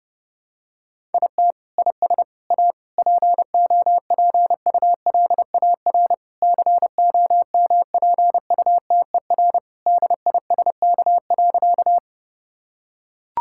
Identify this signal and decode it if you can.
Morse code, heard as a keyed tone, about 30 wpm, 700 Hz. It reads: IT IS A POPULAR COMPUTER DISK.
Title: Sentences from Top 1000 Words - Review Part 6 - 30wpm